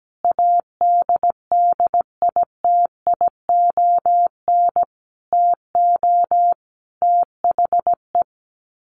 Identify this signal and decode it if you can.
Morse code, heard as a keyed tone, about 17 wpm, 700 Hz. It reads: ADDITION TO THE
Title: Sets of 4 Words 17wpm